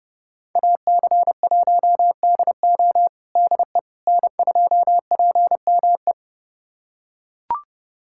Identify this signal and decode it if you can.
Morse code, heard as a keyed tone, about 32 wpm, 700 Hz. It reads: AC1DO DE N2PME
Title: USA Callsign DE Callsign - 5 Characters - 30wpm